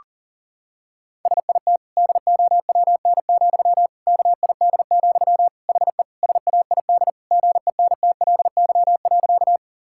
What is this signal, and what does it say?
Morse code, heard as a keyed tone, about 40 wpm, 700 Hz. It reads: SIT DOWN, KID, HE SAID GENTLY.
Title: Sentences from Top 1600 Words 40wpm